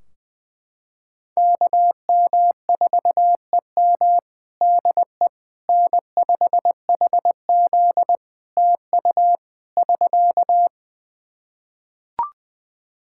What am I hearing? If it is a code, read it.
Morse code, heard as a keyed tone, about 20 wpm, 700 Hz. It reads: KM4EM DE N5HZ TU <SK>
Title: Practice QSO - Exchange 4 - Ending - 20wpm